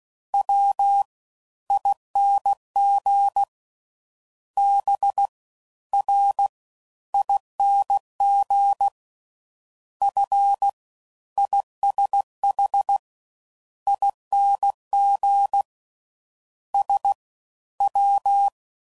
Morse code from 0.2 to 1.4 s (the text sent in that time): W